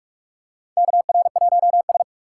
Morse code, heard as a keyed tone, about 46 wpm, 700 Hz. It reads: KR1S
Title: USA Call Signs - 4 Characters - 2x Character Spacing - 45wpm